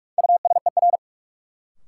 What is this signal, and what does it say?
Morse code, heard as a keyed tone, about 46 wpm, 700 Hz. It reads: USER
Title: Sentences from Top 1000 Words - Review Part 5 - New Word Drill - 45wpm